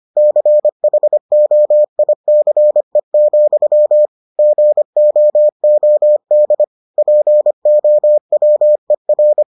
Morse code, heard as a keyed tone, about 25 wpm, 600 Hz. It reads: CHOICE, GOOD POWER